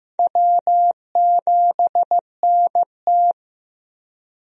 Morse code, heard as a keyed tone, about 15 wpm, 700 Hz. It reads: W7NT
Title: USA Call Signs - 4 Characters - Repeated 3x - 15wpm